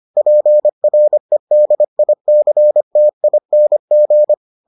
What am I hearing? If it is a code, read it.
Morse code, heard as a keyed tone, about 25 wpm, 600 Hz. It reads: PREDICTING